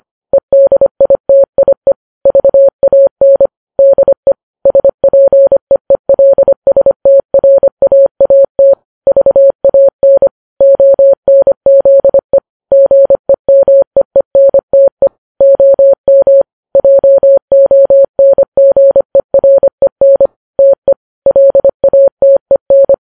Morse code, heard as a keyed tone, about 25 wpm, 550 Hz. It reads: EDITIE VAN DE SPEELSTRAAT VAN ONZE GEMEENTE OM JONGEREN TE LATEN
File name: Files Web tls / Morse on9fv / morse25kps